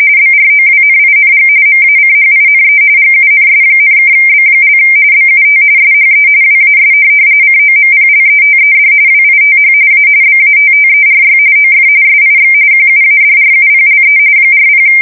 RTTY, about 45.45 baud. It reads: BROWNING MACHINE GUNS,(B) BUT SHE WAS COMPLETED WITH A BATTERY OF SIX QUADRUPLE 40
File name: fsk170 / USS-Massachusetts